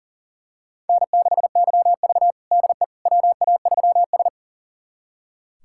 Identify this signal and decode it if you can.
Morse code, heard as a keyed tone, about 40 wpm, 700 Hz. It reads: N6YV DE WA3S
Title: USA Callsign DE Callsign - 4 Characters - 40wpm